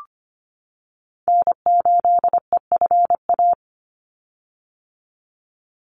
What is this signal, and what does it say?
Morse code, heard as a keyed tone, about 25 wpm, 700 Hz. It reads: N8EFA
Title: USA Call Signs 25wpm